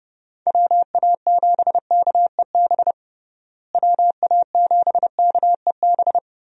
Morse code, heard as a keyed tone, about 30 wpm, 700 Hz. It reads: WA7KEB WA7KEB
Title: USA Call Signs - 6 Characters - Repeated 2x - 30wpm